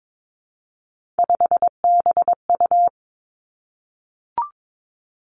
Morse code, heard as a keyed tone, about 22 wpm, 700 Hz. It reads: 5BU
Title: Three Challenging Letter-Number Combinations 22wpm